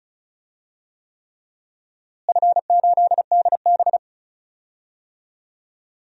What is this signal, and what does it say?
Morse code, heard as a keyed tone, about 36 wpm, 700 Hz. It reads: F8DB
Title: DXCC-100 Call Signs - 4 Characters - Repeated 3x - 35wpm